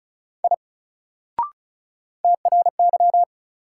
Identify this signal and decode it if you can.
Morse code, heard as a keyed tone, about 35 wpm, 700 Hz. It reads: I TRY
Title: Sentences from Top 200 Words - New Word Drill - 35wpm